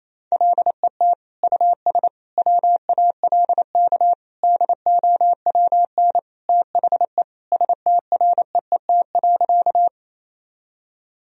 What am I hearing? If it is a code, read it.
Morse code, heard as a keyed tone, about 27 wpm, 700 Hz. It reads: LET US WALK DOWN THE STREET.
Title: Sentences from Top 1000 Words - Review Part 3 - 28wpm